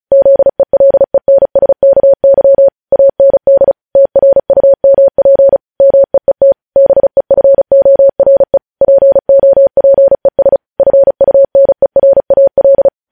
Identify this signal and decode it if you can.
Morse code, heard as a keyed tone, about 36 wpm, 550 Hz. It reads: ZELENSKY AND TRUMP MEET BEFORE POPES FUNERAL
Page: News headlines in Morse code at 35 wpm